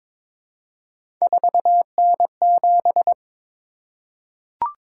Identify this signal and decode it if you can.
Morse code, heard as a keyed tone, about 22 wpm, 700 Hz. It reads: 4N7
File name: practice / 22wpm / Three Letter-Number Combinations - 22wpm